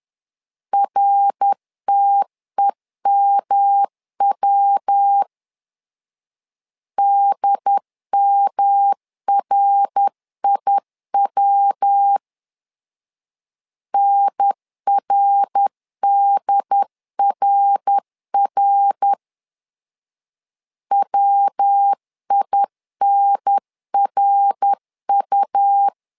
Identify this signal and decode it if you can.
Morse code, heard as a keyed tone, about 10 wpm, 800 Hz. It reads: RTEMW DMRIW NRDRR WINRU
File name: LEZIONI / 3 MWRO / 3 MWR 10wpm 1952